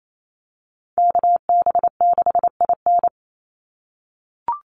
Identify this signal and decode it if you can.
Morse code, heard as a keyed tone, about 28 wpm, 700 Hz. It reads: KB6IN
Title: USA Call Signs 28wpm